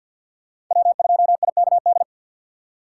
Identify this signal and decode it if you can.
Morse code, heard as a keyed tone, about 52 wpm, 700 Hz. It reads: W2IKD
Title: USA Call Signs - 6x Character Spacing - 50wpm